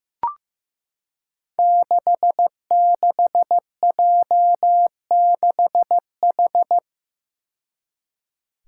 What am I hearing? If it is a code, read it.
Morse code, heard as a keyed tone, about 15 wpm, 700 Hz. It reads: 66J6H
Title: Five Challenging Letter-Number Combinations 15wpm